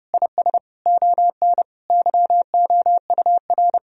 Morse code, heard as a keyed tone, about 32 wpm, 700 Hz. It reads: IS ON YOUR
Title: Sentences from Top 1000 Words - Review Part 3 - 30wpm